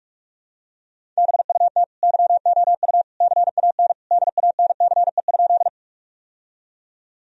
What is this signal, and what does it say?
Morse code, heard as a keyed tone, about 44 wpm, 700 Hz. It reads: BUT YOU CAN DANCE?
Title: Sentences from Top 1000 Words - Review Part 4 - 45wpm